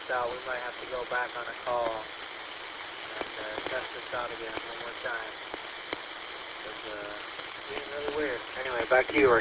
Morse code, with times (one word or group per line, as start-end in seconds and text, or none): none